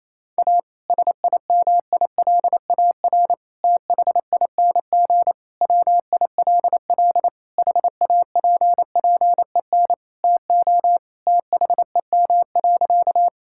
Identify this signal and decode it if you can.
Morse code, heard as a keyed tone, about 28 wpm, 700 Hz. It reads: A SIMILAR THING WILL HAPPEN TO THEM.
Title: Sentences from Top 1000 Words - Review Part 5 - 28wpm